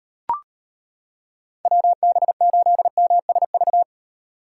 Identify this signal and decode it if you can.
Morse code, heard as a keyed tone, about 33 wpm, 700 Hz. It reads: WB8MSV